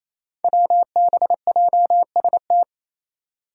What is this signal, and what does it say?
Morse code, heard as a keyed tone, about 28 wpm, 700 Hz. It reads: WBJST
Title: Five Challenging Letter-Number Combinations 28wpm